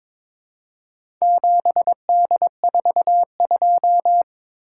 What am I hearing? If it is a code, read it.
Morse code, heard as a keyed tone, about 22 wpm, 700 Hz. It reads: 7D42
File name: practice / 22wpm / Four Letter-Number Combinations - 22wpm